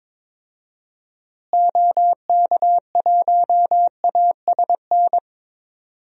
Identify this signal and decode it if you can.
Morse code, heard as a keyed tone, about 22 wpm, 700 Hz. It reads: OK1ASN
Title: DXCC-100 Call Signs - 6 Characters - Repeated 3x - 22wpm